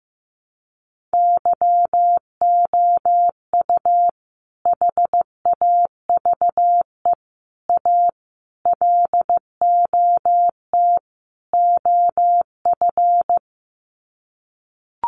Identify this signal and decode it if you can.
Morse code, heard as a keyed tone, about 15 wpm, 700 Hz. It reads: YOU HAVE A LOT OF
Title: Sets of 5 Words Encore 15wpm